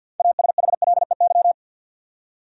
Morse code, heard as a keyed tone, about 52 wpm, 700 Hz. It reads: ASHLEY
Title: USA Common Names - 50wpm